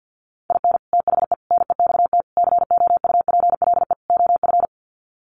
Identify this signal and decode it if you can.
Morse code, heard as a keyed tone, about 50 wpm, 700 Hz. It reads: IN THE NEXT COUPLE OF
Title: Sets of 5 Words Encore2 50wpm